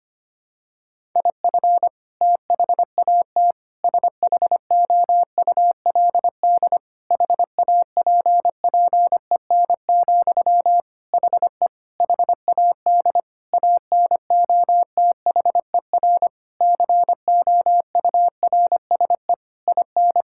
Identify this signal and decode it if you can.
Morse code, heard as a keyed tone, about 25 wpm, 700 Hz. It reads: IF THAT SHOULD HAPPEN, HE HAD ANOTHER COURSE IN
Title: Sentences from Top 1000 Words - Review Part 6 - 25wpm